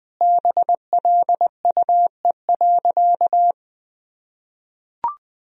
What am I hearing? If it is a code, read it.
Morse code, heard as a keyed tone, about 20 wpm, 700 Hz. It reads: BLUE.
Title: Sentences from Top 1000 Words - Review Part 3 - 20wpm